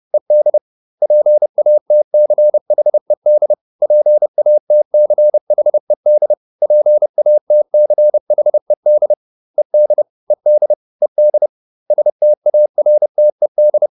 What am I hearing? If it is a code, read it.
Morse code, heard as a keyed tone, about 30 wpm, 600 Hz. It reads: ED PATCHED PATCHED PATCHED ED ED ED STARTED